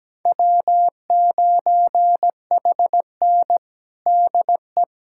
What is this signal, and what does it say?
Morse code, heard as a keyed tone, about 17 wpm, 700 Hz. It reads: W9HN DE
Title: USA Callsign DE Callsign - 4 Characters - 17wpm